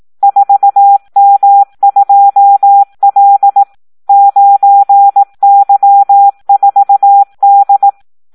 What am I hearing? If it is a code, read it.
Morse code, heard as a keyed tone, about 18 wpm, 800 Hz. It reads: 4M2L 9Y4D
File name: Audio practica CW / SD 1 18